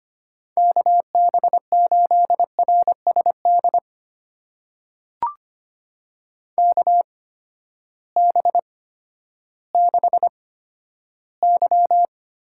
Morse code, heard as a keyed tone, about 25 wpm, spaced as for 13 wpm, 700 Hz. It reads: KB8RSD KB6Y
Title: USA Call Signs - 6 Characters - 8x Character Spacing - 25wpm